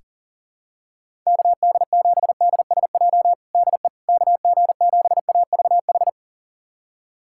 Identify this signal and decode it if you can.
Morse code, heard as a keyed tone, about 40 wpm, 700 Hz. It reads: KD7DSJ DE KG7AVH
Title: USA Callsign DE Callsign 40wpm